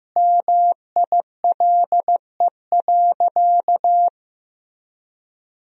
Morse code, heard as a keyed tone, about 15 wpm, 700 Hz. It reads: MILE.
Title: Sentences from Top 1000 Words - Review Part 1 - 15wpm